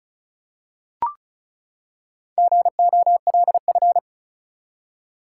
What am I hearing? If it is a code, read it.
Morse code, heard as a keyed tone, about 36 wpm, 700 Hz. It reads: GOLF
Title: Four Letter Words - 35wpm